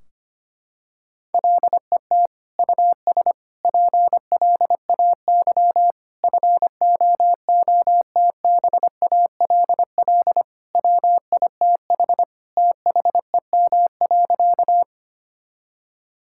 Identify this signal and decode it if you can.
Morse code, heard as a keyed tone, about 25 wpm, 700 Hz. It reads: LET US PLAY FOOTBALL WITH THEM.
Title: Sentences from Top 1600 Words 25wpm